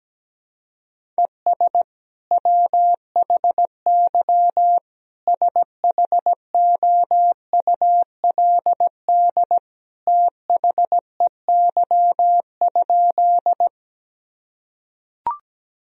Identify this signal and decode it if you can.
Morse code, heard as a keyed tone, about 17 wpm, 700 Hz. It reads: ES WHY SHOULD THEY?
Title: Sentences from Top 300 Words and Abbreviations 17wpm